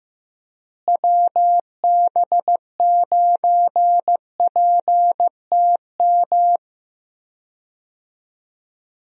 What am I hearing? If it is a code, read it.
Morse code, heard as a keyed tone, about 15 wpm, 700 Hz. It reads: WB9PTM
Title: USA Call Signs - 6 Characters - Repeated 3x - 15wpm